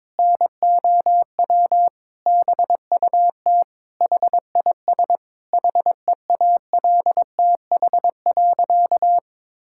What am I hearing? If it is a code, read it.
Morse code, heard as a keyed tone, about 22 wpm, 700 Hz. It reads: NOW BUT HIS HEALTH.
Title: Sentences from Top 1000 Words - Review Part 6 - 22wpm